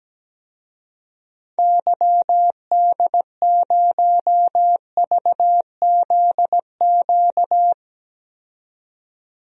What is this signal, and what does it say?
Morse code, heard as a keyed tone, about 17 wpm, 700 Hz. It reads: YD0VZQ
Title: DXCC-100 Call Signs - 6 Characters - 17wpm